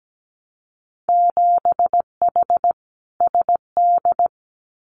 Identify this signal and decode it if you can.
Morse code, heard as a keyed tone, about 17 wpm, 700 Hz. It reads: 7H SD
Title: Winter Field Day Exchanges Encore 17wpm